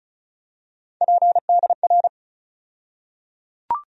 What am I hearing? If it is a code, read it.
Morse code, heard as a keyed tone, about 36 wpm, 700 Hz. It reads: PDR
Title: USA Call Sign Postfixes - 3 Letters 35wpm